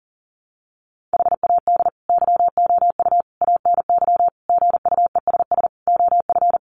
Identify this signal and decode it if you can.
Morse code, heard as a keyed tone, about 37 wpm, 700 Hz. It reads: HAD YOU ANY GUESS OF
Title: Sentences from Top 1500 Words 40wpm